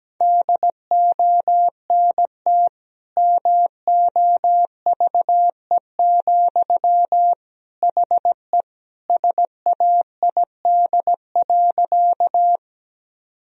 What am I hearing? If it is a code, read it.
Morse code, heard as a keyed tone, about 17 wpm, 700 Hz. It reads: DONT MOVE, HE SAID.
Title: Sentences from Top 1000 Words - Review Part 1 - 17wpm